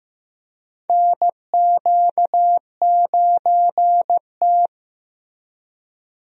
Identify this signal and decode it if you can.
Morse code, heard as a keyed tone, about 15 wpm, 700 Hz. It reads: NQ9T
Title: USA Call Signs - 4 Characters - Repeated 2x - 15wpm